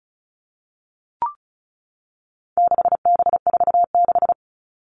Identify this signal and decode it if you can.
Morse code, heard as a keyed tone, about 34 wpm, 700 Hz. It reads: 6B46
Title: All Cut Numbers - 4-Digit 35wpm